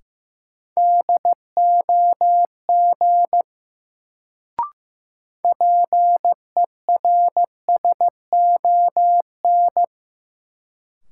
Morse code, heard as a keyed tone, about 15 wpm, 700 Hz. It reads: DOG PERSON
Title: Sentences from Top 400 Words - New Word Drill - 15wpm